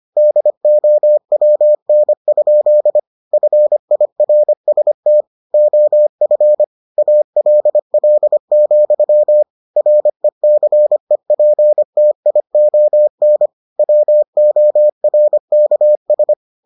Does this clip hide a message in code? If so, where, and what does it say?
Yes: Morse code from 0.2 to 16.3 s, heard as a keyed tone, about 25 wpm, 600 Hz. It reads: DOWN? FIRST OF ALL, RECEPTION WORKS